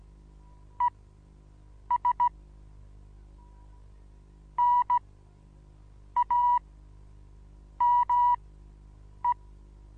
Morse code, heard as a keyed tone, about 16 wpm, spaced as for 7 wpm, 970 Hz. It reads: ES NAME
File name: Onlinelehrgang / morsen / DJ4UF Morsen25